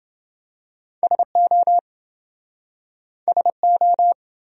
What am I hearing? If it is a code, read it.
Morse code, heard as a keyed tone, about 28 wpm, 700 Hz. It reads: SO SO